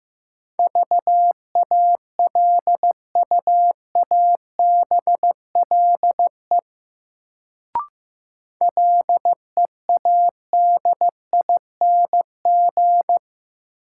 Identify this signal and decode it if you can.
Morse code, heard as a keyed tone, about 15 wpm, 700 Hz. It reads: VALUABLE LEADING
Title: Sentences from Top 1100 Words - New Word Drill Alt - 15wpm